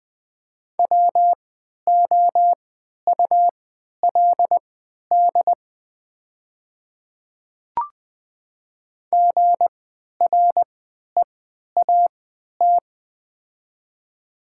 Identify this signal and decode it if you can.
Morse code, heard as a keyed tone, about 20 wpm, spaced as for 11 wpm, 690 Hz. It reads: WOULD GREAT
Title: 3x Character Spacing - Top 300 Words - 20wpm